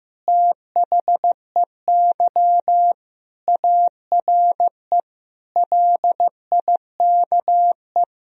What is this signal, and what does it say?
Morse code, heard as a keyed tone, about 15 wpm, 700 Hz. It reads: THEY ARE LIKE
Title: Sentences from Top 1000 Words - Review Part 3 - 15wpm